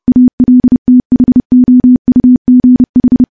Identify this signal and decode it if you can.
Morse code, heard as a keyed tone, about 30 wpm, 250 Hz. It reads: ALTHOUGH
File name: morse / morse-training-30WPM